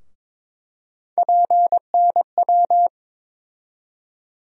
Morse code, heard as a keyed tone, about 22 wpm, 700 Hz. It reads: PNW
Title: USA Call Sign Postfixes - 3 Letters 22wpm